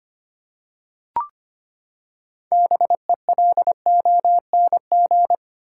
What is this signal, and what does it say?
Morse code, heard as a keyed tone, about 25 wpm, 700 Hz. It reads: BELONG
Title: Six Letter Words - 25wpm